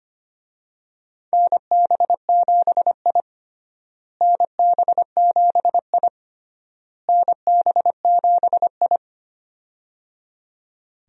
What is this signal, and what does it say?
Morse code, heard as a keyed tone, about 25 wpm, 700 Hz. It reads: NB7I NB7I NB7I
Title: USA Call Signs - 4 Characters - Repeated 3x - 25wpm